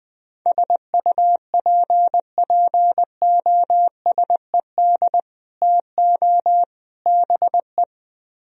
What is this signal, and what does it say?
Morse code, heard as a keyed tone, about 20 wpm, 700 Hz. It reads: SUPPOSED TO BE